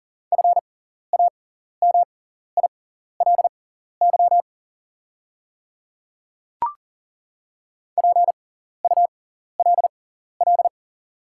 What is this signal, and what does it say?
Morse code, heard as a keyed tone, about 40 wpm, spaced as for 14 wpm, 700 Hz. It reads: FAMILY PULL
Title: 6x Character Spacing - Top 500 Words - 40wpm